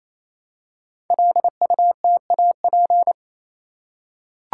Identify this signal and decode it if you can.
Morse code, heard as a keyed tone, about 27 wpm, 690 Hz. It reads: LUTAP
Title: Code Groups - 5 Letters - 28wpm